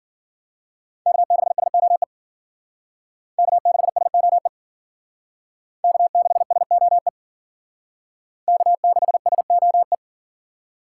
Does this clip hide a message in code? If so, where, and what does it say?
Yes: Morse code from 1.1 to 9.9 s, heard as a keyed tone, about 51 wpm, 700 Hz. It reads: K6SOE K6SOE K6SOE K6SOE